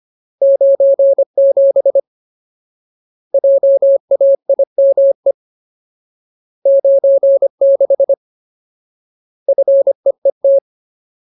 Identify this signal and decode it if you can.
Morse code, heard as a keyed tone, about 25 wpm, 550 Hz. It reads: 97 JAIME 96 FEET
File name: Ultimate Word List / 25WPM Ultimate